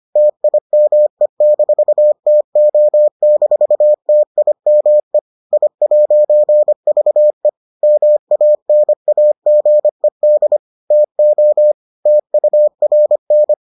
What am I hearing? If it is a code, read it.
Morse code, heard as a keyed tone, about 25 wpm, 600 Hz. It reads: TIME-TO-TIME I'VE MANAGED TO TURN